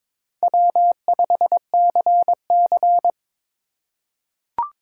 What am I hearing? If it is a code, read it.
Morse code, heard as a keyed tone, about 22 wpm, 700 Hz. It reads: W5CC
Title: USA Call Signs - 8x Character Spacing - 22wpm